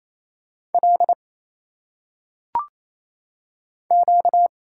Morse code, heard as a keyed tone, about 27 wpm, 700 Hz. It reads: L Q
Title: Single Letters - Phonetic Pronunciation 28wpm